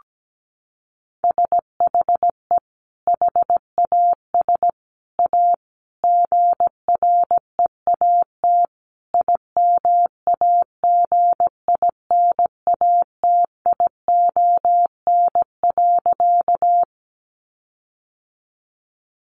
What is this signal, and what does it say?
Morse code, heard as a keyed tone, about 17 wpm, 700 Hz. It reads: SHE HAS A GREAT IMAGINATION.
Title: Sentences from Top 1300 Words 17wpm